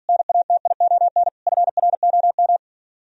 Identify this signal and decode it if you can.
Morse code, heard as a keyed tone, about 60 wpm, 700 Hz. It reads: NATION FROM